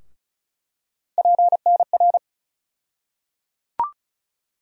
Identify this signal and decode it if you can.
Morse code, heard as a keyed tone, about 36 wpm, 700 Hz. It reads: PNR
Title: USA Call Sign Postfixes - 3 Letters 35wpm